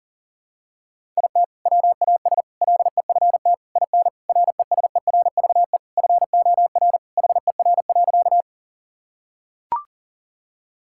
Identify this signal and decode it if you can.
Morse code, heard as a keyed tone, about 40 wpm, 700 Hz. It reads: IT WAS LEFT IN RESERVE FOR HER.
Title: Sentences from Top 1000 Words - Review Part 6 - 40wpm